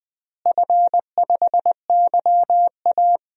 Morse code, heard as a keyed tone, about 20 wpm, 700 Hz. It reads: F5YA
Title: DXCC-100 Call Signs - 4 Characters - Repeated 3x - 20wpm